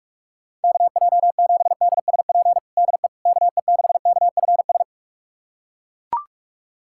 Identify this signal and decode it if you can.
Morse code, heard as a keyed tone, about 41 wpm, 700 Hz. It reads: KJ7DSP DE KE6KFS
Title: USA Callsign DE Callsign - 6 Characters - 45wpm